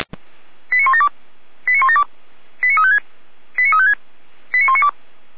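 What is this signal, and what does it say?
Five-tone selective call (ZVEI), heard as quick runs of five tones, tones of about 80 ms. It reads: 89172 89172 89356 89356 89192
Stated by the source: ZVEI five-tone selective call received on an SDR.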